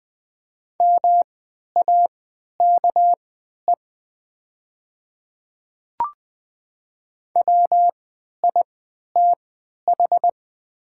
Morse code, heard as a keyed tone, about 20 wpm, spaced as for 11 wpm, 700 Hz. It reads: MAKE WITH
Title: 3x Character Spacing - Top 100 Words - 20wpm